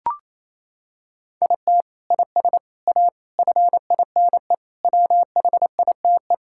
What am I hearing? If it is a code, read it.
Morse code, heard as a keyed tone, about 28 wpm, 700 Hz. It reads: IT IS A FINE WHITE
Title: Sentences from Top 1600 Words 28wpm